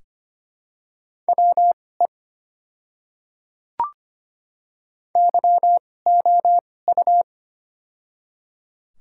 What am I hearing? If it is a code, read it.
Morse code, heard as a keyed tone, about 25 wpm, spaced as for 18 wpm, 700 Hz. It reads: WE YOU
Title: 2x Character Spacing - Top 200 Words - 25wpm